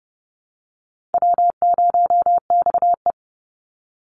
Morse code, heard as a keyed tone, about 30 wpm, 700 Hz. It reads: W0XE
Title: Easy Call Signs - 30wpm